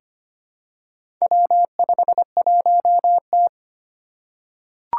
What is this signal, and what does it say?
Morse code, heard as a keyed tone, about 25 wpm, 700 Hz. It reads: W51T
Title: Four Challenging Letter-Number Combinations 25wpm